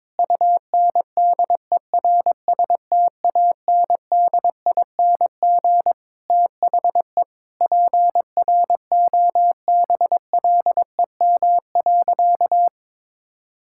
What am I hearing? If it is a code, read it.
Morse code, heard as a keyed tone, about 22 wpm, 700 Hz. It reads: UNDERSTANDING THE PROBLEM.
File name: Sentences with 9-Word Phrases - 22wpm